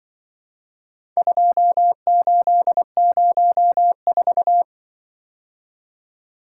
Morse code, heard as a keyed tone, about 23 wpm, 700 Hz. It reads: 2804
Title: Speed-Racing - Four Letter-Number Combinations - 20wpm